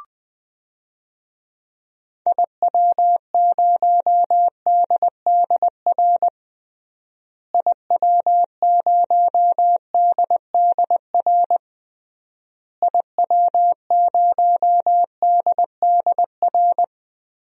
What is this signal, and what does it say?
Morse code, heard as a keyed tone, about 20 wpm, 700 Hz. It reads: IW0DDR IW0DDR IW0DDR
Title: DXCC-100 Call Signs - 6 Characters - Repeated 3x - 20wpm